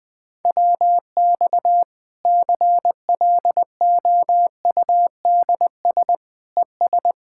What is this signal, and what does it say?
Morse code, heard as a keyed tone, about 21 wpm, 700 Hz. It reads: WX CLOUDS ES
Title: Practice QSO - Exchange 2 - Info - 20wpm